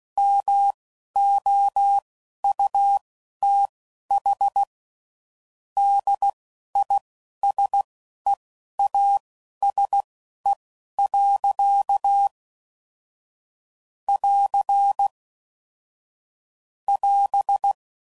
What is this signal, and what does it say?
Morse code, heard as a keyed tone, about 16 wpm, spaced as for 11 wpm, 800 Hz. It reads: MOUTH DISEASE. <AR> <AS>
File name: Sayings / Quotes / HoofInMouth